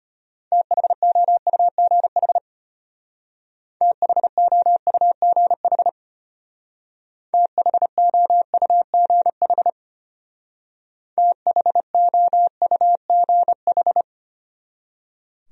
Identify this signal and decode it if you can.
Morse code, heard as a keyed tone, about 33 wpm, 700 Hz. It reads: THOUGH THOUGH THOUGH THOUGH